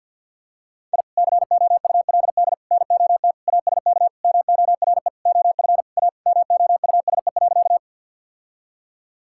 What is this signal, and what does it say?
Morse code, heard as a keyed tone, about 50 wpm, 700 Hz. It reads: I COULD NOT ASK MORE OF A MOUSE.
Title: Sentences from Top 1000 Words - Review Part 5 - 50wpm